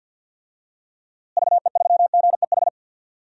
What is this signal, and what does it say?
Morse code, heard as a keyed tone, about 52 wpm, 700 Hz. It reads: VE3GEH
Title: DXCC-100 Call Signs - 6 Characters - Repeated 3x - 50wpm